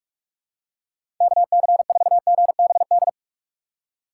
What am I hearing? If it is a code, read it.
Morse code, heard as a keyed tone, about 46 wpm, 700 Hz. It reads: KC4GBD
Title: USA Call Signs - 6 Characters - 6x Character Spacing - 45wpm